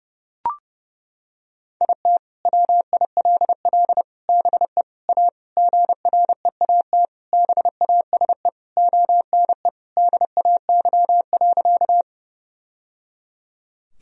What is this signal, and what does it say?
Morse code, heard as a keyed tone, about 30 wpm, 700 Hz. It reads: IT WILL BE A GREAT BASE ONE DAY.
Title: Sentences from Top 1000 Words - Review Part 1 - 30wpm